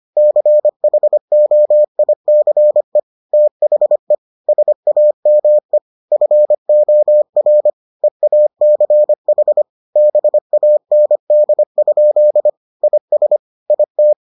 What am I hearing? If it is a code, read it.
Morse code, heard as a keyed tone, about 25 wpm, 600 Hz. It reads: CHOICE THE SAME FOR EACH BAND? IS IT